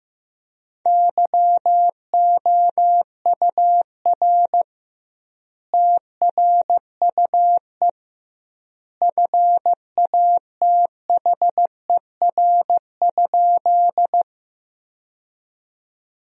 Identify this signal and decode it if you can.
Morse code, heard as a keyed tone, about 15 wpm, 700 Hz. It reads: YOUR TRUE FATHER?
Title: 2x Word-Spacing - Sentences from Top 400 Words - 15wpm